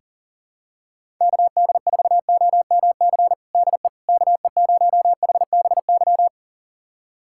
Morse code, heard as a keyed tone, about 40 wpm, 700 Hz. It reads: KD4OMC DE KE0HBY
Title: USA Callsign DE Callsign - 6 Characters - 40wpm